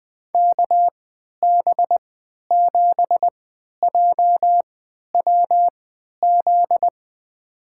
Morse code, heard as a keyed tone, about 20 wpm, spaced as for 11 wpm, 700 Hz. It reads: KB7JWZ
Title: USA Call Signs - 6 Characters - 3x Character Spacing - 20wpm